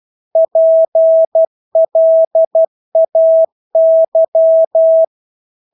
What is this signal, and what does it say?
Morse code, heard as a keyed tone, about 12 wpm, 650 Hz. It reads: PLAY
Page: Engelske ord 12 wpm